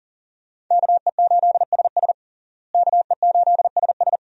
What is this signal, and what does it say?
Morse code, heard as a keyed tone, about 40 wpm, 700 Hz. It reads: KE8SS KE8SS
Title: USA Call Signs - 5 Characters - Repeated 2x - 40wpm